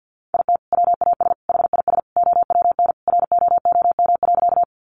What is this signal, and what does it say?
Morse code, heard as a keyed tone, about 50 wpm, 700 Hz. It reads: IT WAS HIS OWN ROOM.
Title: Sentences from Top 300 Words 50wpm